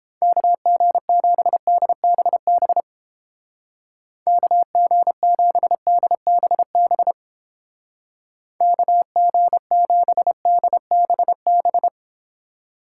Morse code, heard as a keyed tone, about 31 wpm, 700 Hz. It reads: KG7DBB KG7DBB KG7DBB